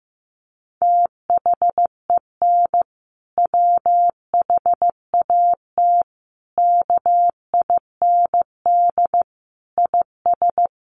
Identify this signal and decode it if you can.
Morse code, heard as a keyed tone, about 15 wpm, 700 Hz. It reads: THEN WHAT KIND IS
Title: Sentences from Top 200 Words 15wpm